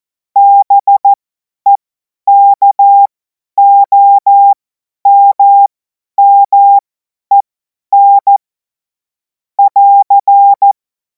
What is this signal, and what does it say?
Morse code, heard as a keyed tone, about 14 wpm, spaced as for 10 wpm, 800 Hz. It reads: BEKOMMEN <AR>
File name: Morsekurs / Witze / Witz 17. Februar 2026 50 F70